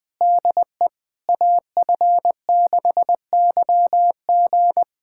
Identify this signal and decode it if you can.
Morse code, heard as a keyed tone, about 20 wpm, 690 Hz. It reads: DE AF6YG
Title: SOTA QSO Practice 20wpm